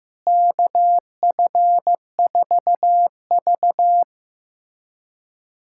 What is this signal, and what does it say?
Morse code, heard as a keyed tone, about 16 wpm, 700 Hz. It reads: KF4V
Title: USA Callsign DE Callsign - 4 Characters - 15wpm